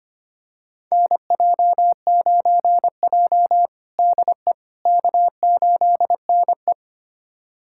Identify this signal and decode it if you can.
Morse code, heard as a keyed tone, about 25 wpm, 700 Hz. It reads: NJ9J DE K8NE
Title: USA Callsign DE Callsign - 4 Characters - 25wpm